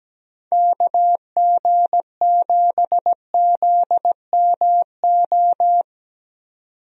Morse code, heard as a keyed tone, about 17 wpm, 700 Hz. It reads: KG7ZMO
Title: USA Call Signs - 6 Characters - 17wpm